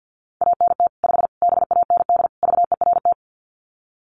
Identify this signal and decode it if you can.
Morse code, heard as a keyed tone, about 50 wpm, 700 Hz. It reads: ANT 5 BAND VERT
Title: Common QSO Phrases 50wpm